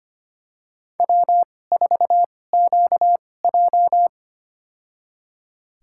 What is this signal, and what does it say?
Morse code, heard as a keyed tone, about 25 wpm, spaced as for 18 wpm, 700 Hz. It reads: W4QJ
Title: USA Call Signs - 4 Characters - 2x Character Spacing - 25wpm